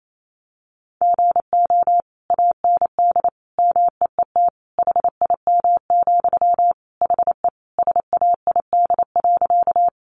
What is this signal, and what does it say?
Morse code, heard as a keyed tone, about 28 wpm, 700 Hz. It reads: GO AND MEET HIM, HE SAID.
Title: Sentences from Top 1100 Words 28wpm